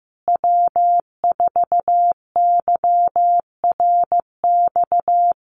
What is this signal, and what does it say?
Morse code, heard as a keyed tone, about 15 wpm, 700 Hz. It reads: W4YRX
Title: Common QSO Phrases Encore 15wpm